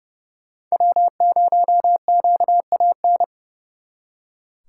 Morse code, heard as a keyed tone, about 36 wpm, 700 Hz. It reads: W0QAN
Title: Speed Racing - USA 1x3 Call Signs 30wpm